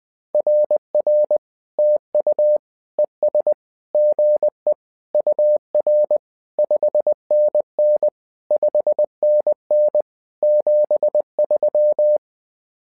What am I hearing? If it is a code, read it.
Morse code, heard as a keyed tone, about 20 wpm, 600 Hz. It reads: RR TU ES GE UR 5NN 5NN 73
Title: SOTA QSO Practice 20wpm